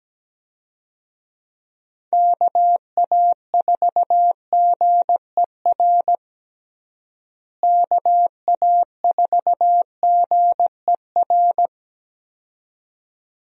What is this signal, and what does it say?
Morse code, heard as a keyed tone, about 17 wpm, 700 Hz. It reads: KA4GER KA4GER
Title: USA Call Signs - 6 Characters - Repeated 2x - 17wpm